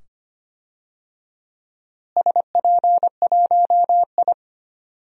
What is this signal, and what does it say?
Morse code, heard as a keyed tone, about 25 wpm, 700 Hz. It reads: SP1I
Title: DXCC-100 Call Signs - 4 Characters - Repeated 2x - 25wpm